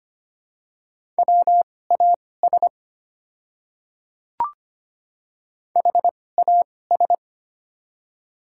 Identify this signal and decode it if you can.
Morse code, heard as a keyed tone, about 25 wpm, spaced as for 18 wpm, 700 Hz. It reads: WAS HAS
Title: 2x Character Spacing - Top 100 Words - 25wpm